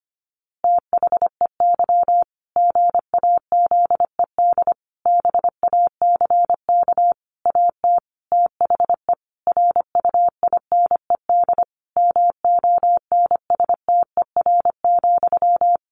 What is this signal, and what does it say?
Morse code, heard as a keyed tone, about 25 wpm, 700 Hz. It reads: THEY GAZED BACK AT THE RUINED MONSTER,
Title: A Sound Of Thunder 25wpm